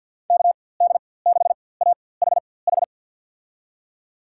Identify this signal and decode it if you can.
Morse code, heard as a keyed tone, about 51 wpm, spaced as for 23 wpm, 700 Hz. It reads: KD6AHH
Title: USA Call Signs - 6 Characters - 4x Character Spacing - 50wpm